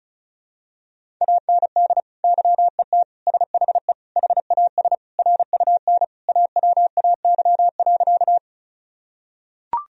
Morse code, heard as a keyed tone, about 36 wpm, 700 Hz. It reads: AND YET SHE HAS RUN AWAY.
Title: Sentences from Top 1000 Words - Review Part 1 - 35wpm